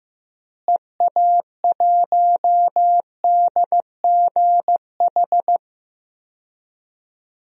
Morse code, heard as a keyed tone, about 15 wpm, 700 Hz. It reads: EA1DGH
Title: DXCC-100 Call Signs - 6 Characters - Repeated 3x - 15wpm